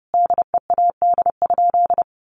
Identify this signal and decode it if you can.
Morse code, heard as a keyed tone, about 30 wpm, 700 Hz. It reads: DEAD?
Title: Sentences from Top 1500 Words 30wpm